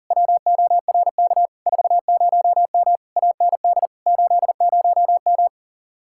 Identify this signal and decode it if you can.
Morse code, heard as a keyed tone, about 49 wpm, 700 Hz. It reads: WORK 40M AND 80M
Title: Speed-Racing Common QSO Phrases Encore - 40wpm